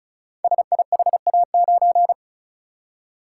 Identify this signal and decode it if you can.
Morse code, heard as a keyed tone, about 36 wpm, 700 Hz. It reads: SIHA9
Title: Code Groups - 5 Letter:Numbers - 35wpm